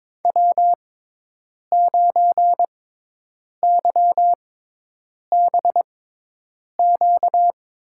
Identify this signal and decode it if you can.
Morse code, heard as a keyed tone, about 22 wpm, spaced as for 8 wpm, 700 Hz. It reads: W9YBQ
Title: USA Call Signs - 6x Character Spacing - 22wpm